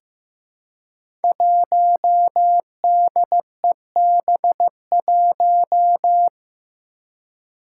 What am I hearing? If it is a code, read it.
Morse code, heard as a keyed tone, about 15 wpm, 700 Hz. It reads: 1DEB1
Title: Five Challenging Letter-Number Combinations 15wpm